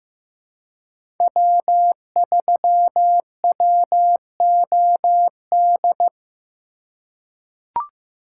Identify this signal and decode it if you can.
Morse code, heard as a keyed tone, about 15 wpm, 700 Hz. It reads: W3WOD
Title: USA Call Signs - 5 Characters - 3x Character Spacing - 15wpm